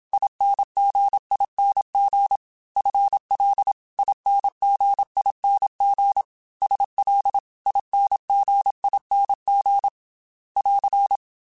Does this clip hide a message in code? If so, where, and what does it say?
Yes: Morse code from 0.1 to 11.2 s, heard as a keyed tone, about 27 wpm, 800 Hz. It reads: INGING FL INGING SL INGING <AR>